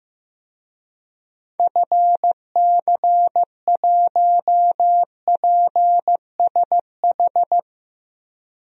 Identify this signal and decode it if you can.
Morse code, heard as a keyed tone, about 15 wpm, 700 Hz. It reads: FC1PSH
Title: DXCC-100 Call Signs - 6 Characters - Repeated 2x - 15wpm